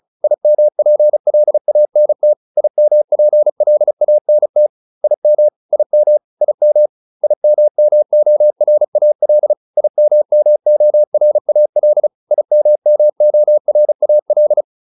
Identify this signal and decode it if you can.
Morse code, heard as a keyed tone, about 34 wpm, 600 Hz. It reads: IMPLANT IMPLANT IM IM IM IMMORAL IMMORAL IMMORAL